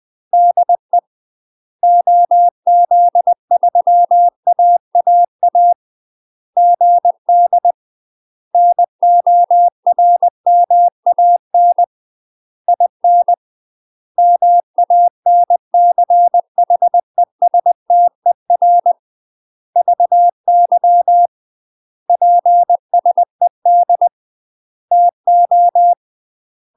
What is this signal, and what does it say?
Morse code, heard as a keyed tone, about 20 wpm, 700 Hz. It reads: DE OZ3AAA GD NORMAN IN MANCHESTER VY PSED TO
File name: UK MASTER QSOs TRAINING / 20WPM / UK MASTER QSOs 0042